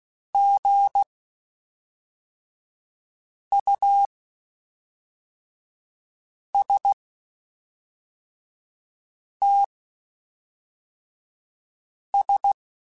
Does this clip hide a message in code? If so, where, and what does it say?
Yes: Morse code from 0.3 to 12.5 s, heard as a keyed tone, about 16 wpm, spaced as for 3 wpm, 800 Hz. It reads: GUSTS